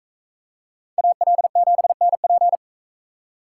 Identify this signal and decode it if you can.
Morse code, heard as a keyed tone, about 41 wpm, 700 Hz. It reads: AL7NP